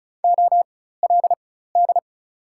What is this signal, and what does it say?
Morse code, heard as a keyed tone, about 36 wpm, spaced as for 16 wpm, 700 Hz. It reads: OLD
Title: 4x Character Spacing - Top 200 Words - 35wpm